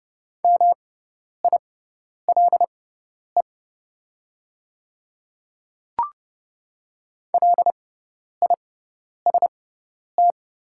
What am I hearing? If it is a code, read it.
Morse code, heard as a keyed tone, about 30 wpm, spaced as for 10 wpm, 690 Hz. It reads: MILE LIST
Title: 6x Character Spacing - Top 500 Words - 30wpm